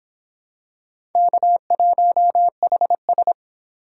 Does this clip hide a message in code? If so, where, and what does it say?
Yes: Morse code from 1.1 to 3.3 s, heard as a keyed tone, about 26 wpm, 700 Hz. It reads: K1HS